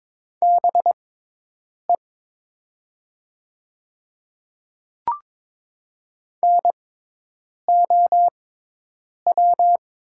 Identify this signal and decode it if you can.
Morse code, heard as a keyed tone, about 22 wpm, spaced as for 8 wpm, 700 Hz. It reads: BE NOW
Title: 6x Character Spacing - Top 100 Words - 22wpm